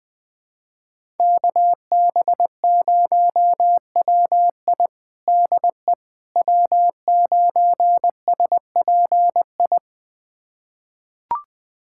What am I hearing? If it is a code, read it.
Morse code, heard as a keyed tone, about 21 wpm, 700 Hz. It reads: KB0WI DE W9SPI
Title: USA Callsign DE Callsign - 5 Characters - 20wpm